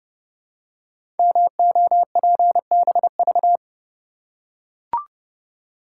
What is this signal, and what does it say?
Morse code, heard as a keyed tone, about 30 wpm, 700 Hz. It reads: MOPBV
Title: Code Groups - 5 Letters - 30wpm